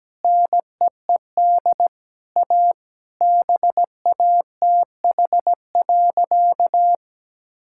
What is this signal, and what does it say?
Morse code, heard as a keyed tone, about 17 wpm, 700 Hz. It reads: NEED A BATH.
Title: Sentences from Top 1600 Words 17wpm